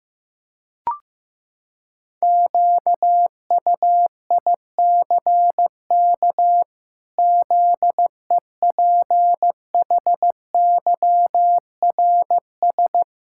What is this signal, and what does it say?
Morse code, heard as a keyed tone, about 15 wpm, 700 Hz. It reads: QUICK ZEPHYRS
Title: Pangram Challenge v2 - 15wpm